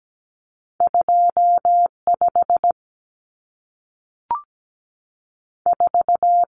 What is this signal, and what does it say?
Morse code, heard as a keyed tone, about 17 wpm, 700 Hz. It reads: 25 4
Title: Common Cut Numbers - 2-Digit 17wpm